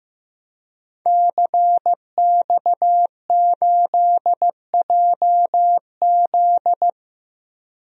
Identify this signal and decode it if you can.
Morse code, heard as a keyed tone, about 15 wpm, 700 Hz. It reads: CX8JZ
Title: DXCC-100 Call Signs - 5 Characters - 15wpm